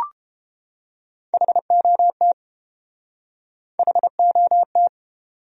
Morse code, heard as a keyed tone, about 29 wpm, 700 Hz. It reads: HOT HOT